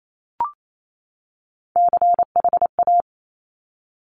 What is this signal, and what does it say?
Morse code, heard as a keyed tone, about 28 wpm, 700 Hz. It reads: CHA
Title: Top Three Letter Combinations in Words 28wpm